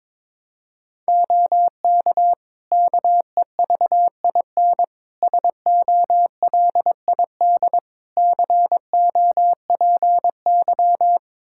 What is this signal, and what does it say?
Morse code, heard as a keyed tone, about 22 wpm, 700 Hz. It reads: OK KEVIN SOLID COPY
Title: Practice QSO - Exchange 3 - More Info - 22wpm